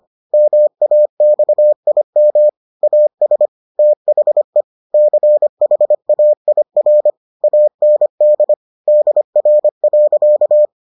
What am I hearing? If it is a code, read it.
Morse code, heard as a keyed tone, about 25 wpm, 600 Hz. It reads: MAXIM AS THE CHAIR AND DR.